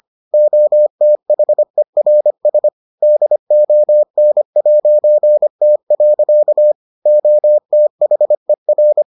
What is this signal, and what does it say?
Morse code, heard as a keyed tone, about 25 wpm, 600 Hz. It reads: OTHERS DON'T. OTHER